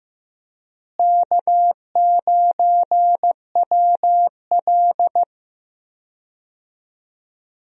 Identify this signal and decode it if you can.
Morse code, heard as a keyed tone, about 15 wpm, 700 Hz. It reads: K9WL
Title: USA Call Signs - 4 Characters - Repeated 2x - 15wpm